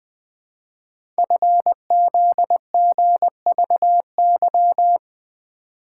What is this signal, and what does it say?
Morse code, heard as a keyed tone, about 20 wpm, 700 Hz. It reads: FZGVY
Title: Code Groups - 5 Letters - 20wpm